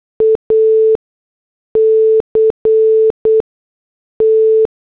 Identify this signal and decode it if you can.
Morse code, heard as a keyed tone, about 8 wpm, spaced as for 6 wpm, 440 Hz. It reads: ACT
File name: morse / headlines